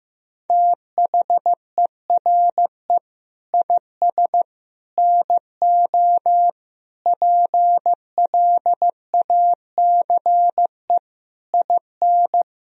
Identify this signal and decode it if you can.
Morse code, heard as a keyed tone, about 15 wpm, 700 Hz. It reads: THERE IS NO PLACE IN
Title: Sentences from Top 1000 Words - Review Part 4 - 15wpm